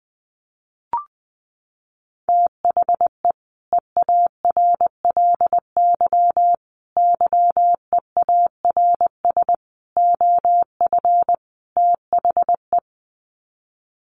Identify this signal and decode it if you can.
Morse code, heard as a keyed tone, about 20 wpm, 700 Hz. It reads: THE EARLY YEARS OF THE
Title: Sets of 5 Words Encore 20wpm